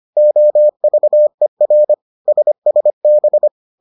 Morse code, heard as a keyed tone, about 25 wpm, 600 Hz. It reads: OVER SSB